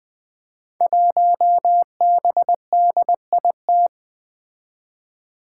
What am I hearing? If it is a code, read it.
Morse code, heard as a keyed tone, about 20 wpm, 700 Hz. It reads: 1BDIT
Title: Five Challenging Letter-Number Combinations 20wpm